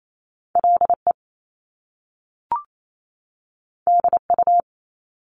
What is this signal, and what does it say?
Morse code, heard as a keyed tone, about 28 wpm, 700 Hz. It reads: L E D U
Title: Top Two Letter Combinations in Words 28wpm